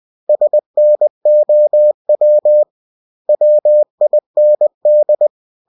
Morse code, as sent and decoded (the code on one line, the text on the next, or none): ... -. --- .-- / .-- .. -. -..
SNOW WIND